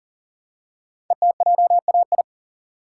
Morse code, heard as a keyed tone, about 39 wpm, 690 Hz. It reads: ETJAI
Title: Code Groups - 5 Letters - 40wpm